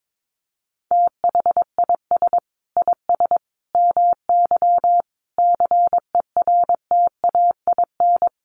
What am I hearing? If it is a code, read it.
Morse code, heard as a keyed tone, about 22 wpm, 700 Hz. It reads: THIS IS MY CERTAIN
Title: Sentences from Top 1500 Words 22wpm